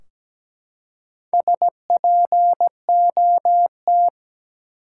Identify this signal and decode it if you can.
Morse code, heard as a keyed tone, about 17 wpm, 700 Hz. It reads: SPOT
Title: Four Letter Words - 17wpm